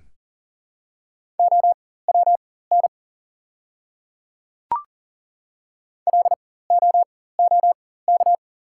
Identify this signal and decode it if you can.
Morse code, heard as a keyed tone, about 40 wpm, spaced as for 19 wpm, 690 Hz. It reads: OWN LOOK
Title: 4x Character Spacing - Top 300 Words - 40wpm